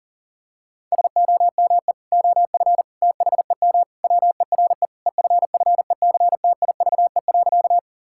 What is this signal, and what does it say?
Morse code, heard as a keyed tone, about 40 wpm, 700 Hz. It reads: SOME OF THEM WERE EFFECTIVE.
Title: Sentences from Top 1000 Words - Review Part 6 - 40wpm